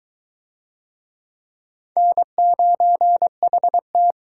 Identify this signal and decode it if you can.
Morse code, heard as a keyed tone, about 22 wpm, 700 Hz. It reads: N9HT